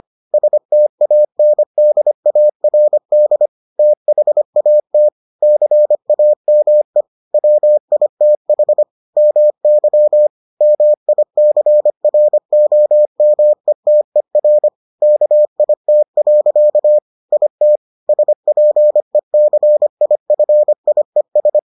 Morse code, heard as a keyed tone, about 25 wpm, 600 Hz. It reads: STANDARD THAT CAME WITH MY MICROMETER KIT. IT SPECIFIES